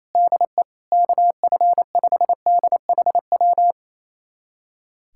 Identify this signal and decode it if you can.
Morse code, heard as a keyed tone, about 28 wpm, 700 Hz. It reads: DE KF5DHW
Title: USA Callsign DE Callsign - 6 Characters - 28wpm